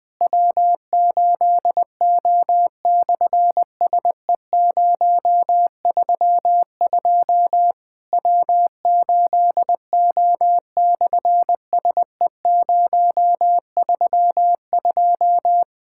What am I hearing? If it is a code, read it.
Morse code, heard as a keyed tone, about 20 wpm, 700 Hz. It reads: W8O/SE032 W8O/SE032
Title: SOTA QSO Practice 20wpm